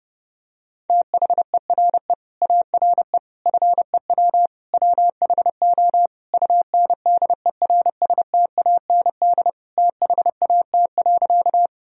Morse code, heard as a keyed tone, about 31 wpm, 700 Hz. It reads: THERE ARE FEW WHO UNDERSTAND THAT.
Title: Sentences from Top 1000 Words - Review Part 1 - 30wpm